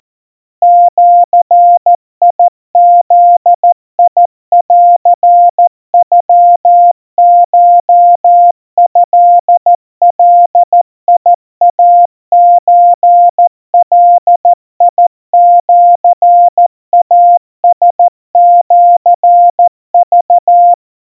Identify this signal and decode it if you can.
Morse code (Japanese Wabun code), heard as a keyed tone, about 14 wpm, 700 Hz. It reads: ジブンノコトガイソガシイラシク
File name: kawa / Morse sample / kamyu 50 cpm 700 Hz Crispy